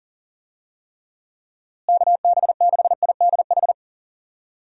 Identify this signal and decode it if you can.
Morse code, heard as a keyed tone, about 40 wpm, 700 Hz. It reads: KB6IDH
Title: USA Call Signs - 6 Characters - Repeated 2x - 40wpm